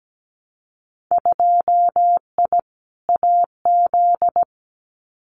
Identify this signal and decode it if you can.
Morse code, heard as a keyed tone, about 17 wpm, 700 Hz. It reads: 2I AZ
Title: Winter Field Day Exchanges Encore 17wpm